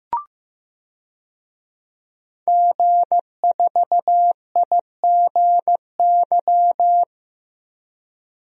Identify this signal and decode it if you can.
Morse code, heard as a keyed tone, about 15 wpm, 700 Hz. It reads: G4IGY
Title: DXCC-100 Call Signs - 5 Characters - Repeated 2x - 15wpm